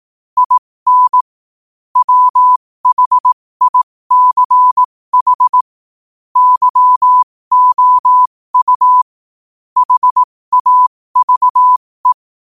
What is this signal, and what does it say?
Morse code, heard as a keyed tone, about 18 wpm, spaced as for 15 wpm, 1000 Hz. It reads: IN WHICH YOU HAVE
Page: New quotes every day in morse code at 15 Words per minute.